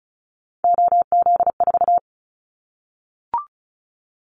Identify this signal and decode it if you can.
Morse code, heard as a keyed tone, about 34 wpm, 700 Hz. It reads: OZ4
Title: Callsign Prefixes for Top 50 DXCC - 35wpm